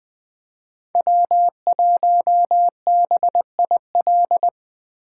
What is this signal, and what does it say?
Morse code, heard as a keyed tone, about 21 wpm, 700 Hz. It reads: W1BIL
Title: Speed Racing - USA 1x3 Call Signs 17wpm